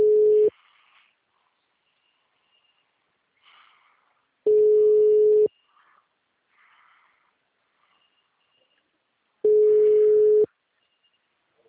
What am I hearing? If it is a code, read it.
Ringback tone (Europe, 425 Hz).